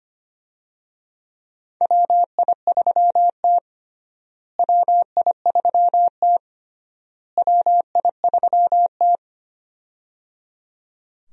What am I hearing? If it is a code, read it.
Morse code, heard as a keyed tone, about 25 wpm, 700 Hz. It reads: WI3T WI3T WI3T
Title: USA Call Signs - 4 Characters - Repeated 3x - 25wpm